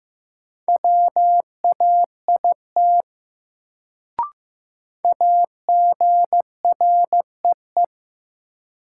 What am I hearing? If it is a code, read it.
Morse code, heard as a keyed tone, about 15 wpm, 700 Hz. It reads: WAIT AGREE
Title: Sets of 5 Words Encore - New Word Drill - 15wpm